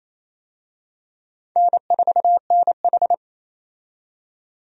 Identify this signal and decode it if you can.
Morse code, heard as a keyed tone, about 28 wpm, 700 Hz. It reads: N4NH
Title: USA Call Signs - 4 Characters - 28wpm